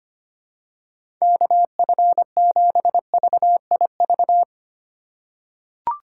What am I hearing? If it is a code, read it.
Morse code, heard as a keyed tone, about 25 wpm, 700 Hz. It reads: KF7VIV
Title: USA Call Signs - 6 Characters - 4x Character Spacing - 25wpm